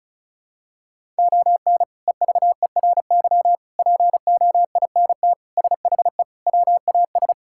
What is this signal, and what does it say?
Morse code, heard as a keyed tone, about 34 wpm, 700 Hz. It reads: ON EVERY POINT SHE WAS
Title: Sentences from Top 1000 Words - Review Part 6 - 35wpm